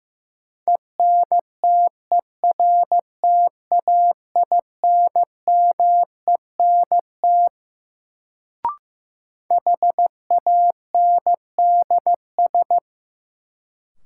Morse code, heard as a keyed tone, about 15 wpm, 700 Hz. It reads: ENTERTAINMENT HANDS
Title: Sentences from Top 1000 Words - Review Part 6 - New Word Drill - 15wpm